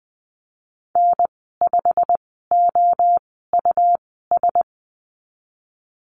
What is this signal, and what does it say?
Morse code, heard as a keyed tone, about 20 wpm, spaced as for 15 wpm, 700 Hz. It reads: N5OUS
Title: 122 Lesson 20g - Number 3 - Call Signs 20wpm